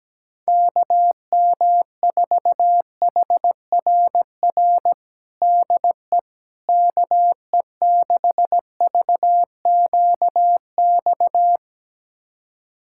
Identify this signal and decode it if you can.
Morse code, heard as a keyed tone, about 17 wpm, 700 Hz. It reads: KM4HRR DE KE6VQX
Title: USA Callsign DE Callsign 17wpm